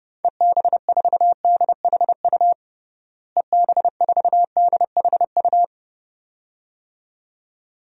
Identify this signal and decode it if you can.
Morse code, heard as a keyed tone, about 30 wpm, 700 Hz. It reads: EB4DHU EB4DHU
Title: DXCC-100 Call Signs - 6 Characters - Repeated 2x - 30wpm